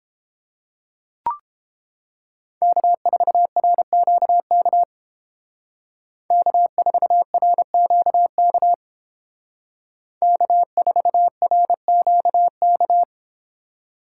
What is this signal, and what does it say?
Morse code, heard as a keyed tone, about 28 wpm, 700 Hz. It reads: K4RQK K4RQK K4RQK